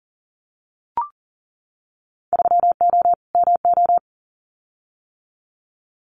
Morse code, heard as a keyed tone, about 40 wpm, 700 Hz. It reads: 3O MO
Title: Winter Field Day Exchanges Encore 40wpm